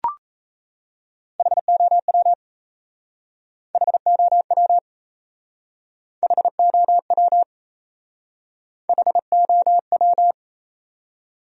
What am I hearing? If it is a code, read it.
Morse code, heard as a keyed tone, about 35 wpm, 700 Hz. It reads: HOW HOW HOW HOW
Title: Speed-Racing Top 100 Words 28wpm